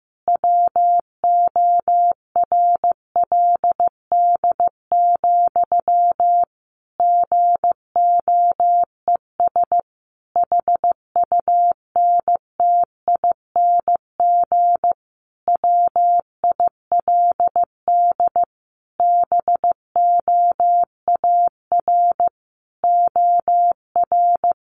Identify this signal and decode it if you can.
Morse code, heard as a keyed tone, about 15 wpm, 700 Hz. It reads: WORLD, GOES HUNTING WILD BOAR OR
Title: A Sound Of Thunder 15wpm